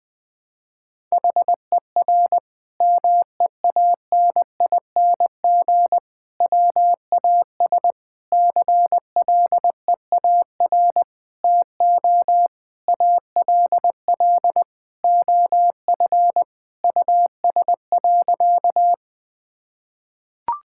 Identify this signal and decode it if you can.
Morse code, heard as a keyed tone, about 21 wpm, 700 Hz. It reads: HER MEANING WAS CLEAR TO ALL OF US.
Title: Sentences from Top 1000 Words - Review Part 6 - 20wpm